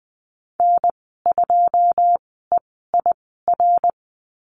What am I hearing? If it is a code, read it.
Morse code, heard as a keyed tone, about 20 wpm, spaced as for 15 wpm, 700 Hz. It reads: N2EIR
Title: 082 Lesson 14g - Number 2 - Call Signs 20wpm